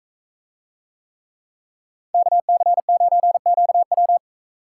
Morse code, heard as a keyed tone, about 42 wpm, 700 Hz. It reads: KC9QW